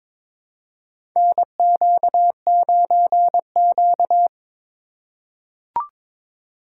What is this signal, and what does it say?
Morse code, heard as a keyed tone, about 22 wpm, 700 Hz. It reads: NQ9Q
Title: USA Call Signs - 4 Characters - 4x Character Spacing - 22wpm